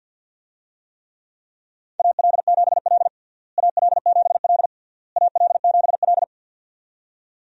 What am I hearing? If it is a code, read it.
Morse code, heard as a keyed tone, about 51 wpm, 700 Hz. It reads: AL7L AL7L AL7L
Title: USA Call Signs - 4 Characters - Repeated 3x - 50wpm